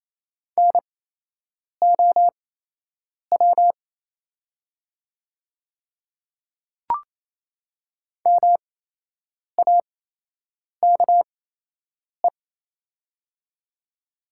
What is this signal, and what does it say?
Morse code, heard as a keyed tone, about 28 wpm, spaced as for 8 wpm, 700 Hz. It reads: NOW MAKE
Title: 8x Character Spacing - Top 200 Words - 28wpm